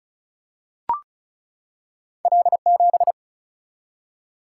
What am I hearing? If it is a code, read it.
Morse code, heard as a keyed tone, about 36 wpm, 700 Hz. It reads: L 7
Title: Single-Letter Single-Number 35wpm